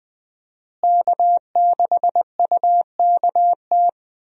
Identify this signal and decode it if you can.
Morse code, heard as a keyed tone, about 20 wpm, 700 Hz. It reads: K6UKT
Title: USA Call Signs - 5 Characters - Repeated 3x - 20wpm